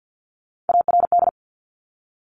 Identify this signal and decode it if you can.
Morse code, heard as a keyed tone, about 50 wpm, 700 Hz. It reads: ARD
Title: Top Three Letter Combinations in Words 50wpm